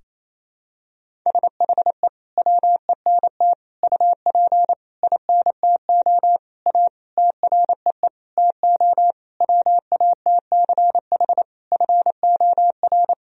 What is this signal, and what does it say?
Morse code, heard as a keyed tone, about 28 wpm, 700 Hz. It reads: SHE WENT UP INTO A TREE TO WATCH FOR
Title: Sentences from Top 1000 Words - Review Part 1 - 28wpm